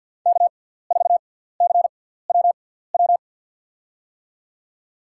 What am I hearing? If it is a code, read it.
Morse code, heard as a keyed tone, about 53 wpm, 700 Hz. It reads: K 4 C W W
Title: USA Call Signs - 6x Character Spacing - 50wpm